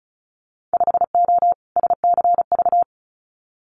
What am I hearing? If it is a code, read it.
Morse code, heard as a keyed tone, about 36 wpm, 700 Hz. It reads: 5O SCV
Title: Winter Field Day Exchanges Encore 35wpm